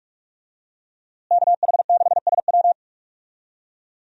Morse code, heard as a keyed tone, about 41 wpm, 700 Hz. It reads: KH6SW
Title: DXCC-100 Call Signs - 5 Characters - Repeated 2x - 45wpm